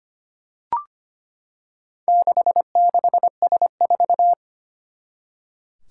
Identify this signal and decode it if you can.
Morse code, heard as a keyed tone, about 25 wpm, 700 Hz. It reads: 66S4
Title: Four Challenging Letter-Number Combinations 25wpm